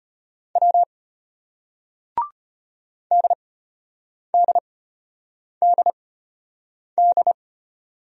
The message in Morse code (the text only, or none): W DDDD